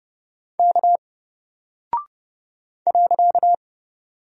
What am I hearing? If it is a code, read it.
Morse code, heard as a keyed tone, about 30 wpm, 700 Hz. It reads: K .
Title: Single Character (Letter:Number:Punctuation:Prosign) Alt 30wpm